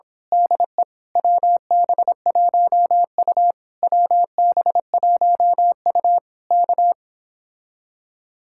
Morse code, heard as a keyed tone, about 26 wpm, 700 Hz. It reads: DE WB1U WB1U K
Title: Speed-Racing Common QSO Phrases Encore - 22wpm